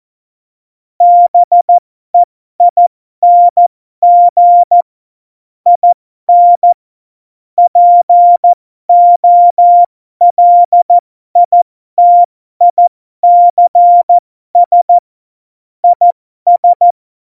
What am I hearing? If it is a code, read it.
Morse code, heard as a keyed tone, about 14 wpm, spaced as for 12 wpm, 700 Hz. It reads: BEING IN POLITICS IS
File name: Morsekurs / Witze